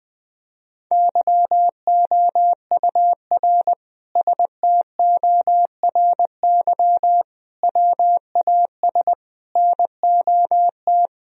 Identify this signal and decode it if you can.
Morse code, heard as a keyed tone, about 20 wpm, 700 Hz. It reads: YOUR STORY WAS NOT
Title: Sentences from Top 1000 Words - Review Part 2 - 20wpm